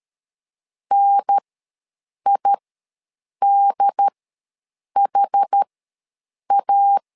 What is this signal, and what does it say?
Morse code, heard as a keyed tone, about 13 wpm, spaced as for 7 wpm, 800 Hz. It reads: NIDHA
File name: LEZIONI / 2 UHDN / 2 UHDN ws 12wpm 1708